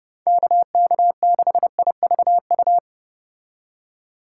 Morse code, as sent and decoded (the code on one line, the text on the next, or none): -.- -.- -.... .. ...- ..-
KK6IVU